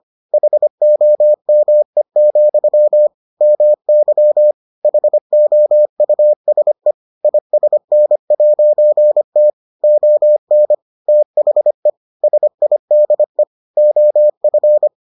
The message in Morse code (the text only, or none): HOME, MY HOUSE ISN'T ON THE SIDE OF